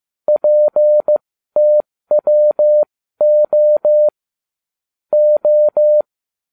Morse code, heard as a keyed tone, about 15 wpm, spaced as for 13 wpm, 600 Hz. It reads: PTWO O